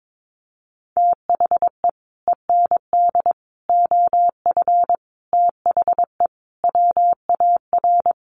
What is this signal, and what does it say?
Morse code, heard as a keyed tone, about 22 wpm, 700 Hz. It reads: THE END OF THE WAR
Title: Sets of 5 Words 22wpm